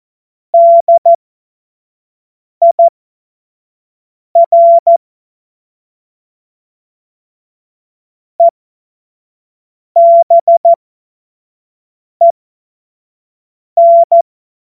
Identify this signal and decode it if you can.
Morse code, heard as a keyed tone, about 14 wpm, spaced as for 5 wpm, 680 Hz. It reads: DIR EBEN
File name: Morsekurs / Witze / Witz 28. März 2023 25 F70